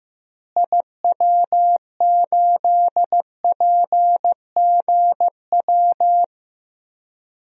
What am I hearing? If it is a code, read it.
Morse code, heard as a keyed tone, about 15 wpm, 700 Hz. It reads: IW8PGW
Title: DXCC-100 Call Signs - 6 Characters - Repeated 2x - 15wpm